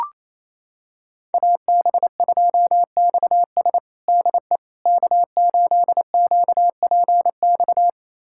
Morse code, heard as a keyed tone, about 28 wpm, 700 Hz. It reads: AB2XS DE K8QPX
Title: USA Callsign DE Callsign - 5 Characters - 28wpm